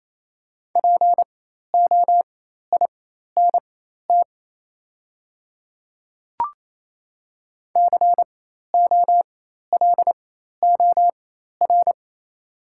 Morse code, heard as a keyed tone, about 28 wpm, spaced as for 13 wpm, 700 Hz. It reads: POINT COLOR
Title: 4x Character Spacing - Top 500 Words - 28wpm